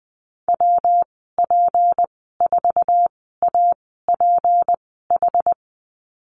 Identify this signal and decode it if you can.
Morse code, heard as a keyed tone, about 20 wpm, spaced as for 14 wpm, 700 Hz. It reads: WP4APH
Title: 161 Lesson 26g - Letter g - Call Signs 20wpm